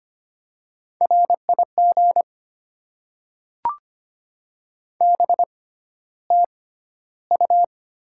Morse code, heard as a keyed tone, about 25 wpm, spaced as for 13 wpm, 700 Hz. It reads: RIG BTU
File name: Top 20 QSO Elements - 6x Character Spacing - 25wpm